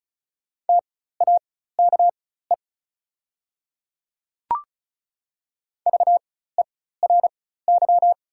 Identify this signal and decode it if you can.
Morse code, heard as a keyed tone, about 36 wpm, spaced as for 16 wpm, 700 Hz. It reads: TAKE VERY
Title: 4x Character Spacing - Top 200 Words - 35wpm